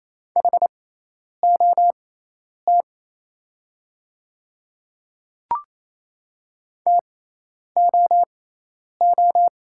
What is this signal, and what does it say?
Morse code, heard as a keyed tone, about 27 wpm, spaced as for 10 wpm, 690 Hz. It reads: HOT TOO
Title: 6x Character Spacing - Top 500 Words - 28wpm